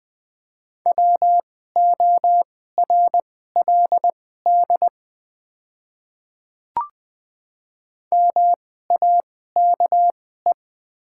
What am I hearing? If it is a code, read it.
Morse code, heard as a keyed tone, about 20 wpm, spaced as for 14 wpm, 700 Hz. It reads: WORLD MAKE
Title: 2x Character Spacing - Top 200 Words - 20wpm